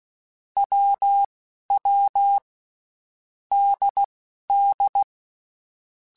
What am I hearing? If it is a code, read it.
Morse code, heard as a keyed tone, about 16 wpm, spaced as for 11 wpm, 800 Hz. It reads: WW DD
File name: Basic Letter Groups By7Groups / AWUNDG / AWUNDG@10